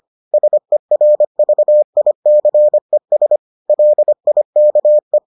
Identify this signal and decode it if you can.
Morse code, heard as a keyed tone, about 25 wpm, 600 Hz. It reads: SERVICES LIKE